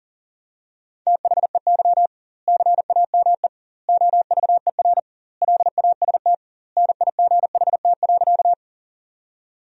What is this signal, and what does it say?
Morse code, heard as a keyed tone, about 40 wpm, 700 Hz. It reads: THEY CAME OVER LAST NIGHT.
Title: 2x Word-Spacing - Sentences from Top 300 Words - 40wpm